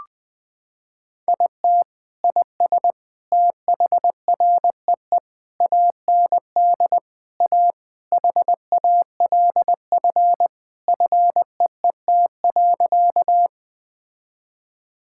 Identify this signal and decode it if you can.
Morse code, heard as a keyed tone, about 21 wpm, 700 Hz. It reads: IT IS THREE AND A HALF FEET.
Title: Sentences with 3-Word Phrases - 20wpm